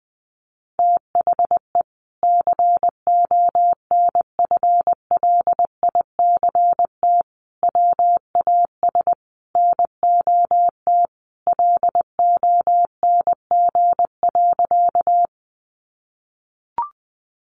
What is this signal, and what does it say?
Morse code, heard as a keyed tone, about 20 wpm, 700 Hz. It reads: THE CONFLICT WAS NOT LONG.
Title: Sentences from Top 1300 Words 20wpm